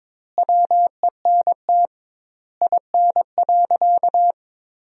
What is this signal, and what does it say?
Morse code, heard as a keyed tone, about 22 wpm, 700 Hz. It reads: WENT IN.
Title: 2x Word-Spacing - Sentences from Top 300 Words - 22wpm